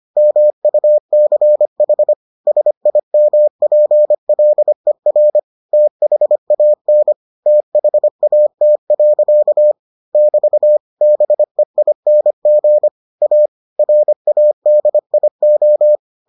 Morse code, heard as a keyed tone, about 25 wpm, 600 Hz. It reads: MUCH SIMPLER THAN THAT. <BT> BEING A RADIO